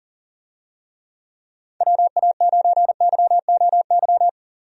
Morse code, heard as a keyed tone, about 40 wpm, 700 Hz. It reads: WA9YOY
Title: USA Call Signs - 6 Characters - Repeated 2x - 40wpm